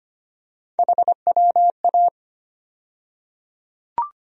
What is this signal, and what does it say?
Morse code, heard as a keyed tone, about 25 wpm, 700 Hz. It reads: HWA
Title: USA Call Sign Postfixes - 3 Letters 25wpm